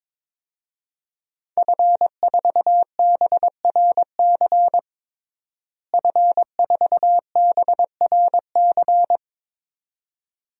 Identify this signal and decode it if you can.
Morse code, heard as a keyed tone, about 22 wpm, 700 Hz. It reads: F4BRC F4BRC
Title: DXCC-100 Call Signs - 5 Characters - Repeated 2x - 22wpm